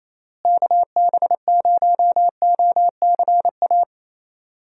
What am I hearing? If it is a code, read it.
Morse code, heard as a keyed tone, about 28 wpm, 700 Hz. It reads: KB0OCA
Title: USA Call Signs - 6 Characters - Repeated 2x - 28wpm